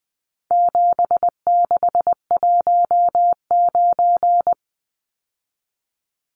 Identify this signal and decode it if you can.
Morse code, heard as a keyed tone, about 20 wpm, 700 Hz. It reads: 7619
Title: Numbers - 4-Digit 20wpm